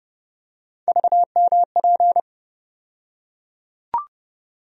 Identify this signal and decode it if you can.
Morse code, heard as a keyed tone, about 30 wpm, 700 Hz. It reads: VMP
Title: USA Call Sign Postfixes - 3 Letters 30wpm